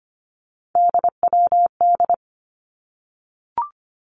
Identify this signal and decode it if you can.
Morse code, heard as a keyed tone, about 25 wpm, 700 Hz. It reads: DWD
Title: Three Challenging Letter-Number Combinations 25wpm